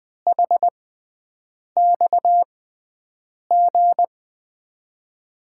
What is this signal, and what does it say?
Morse code, heard as a keyed tone, about 20 wpm, spaced as for 7 wpm, 700 Hz. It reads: HXG
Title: USA Call Signs - 6x Character Spacing - 20wpm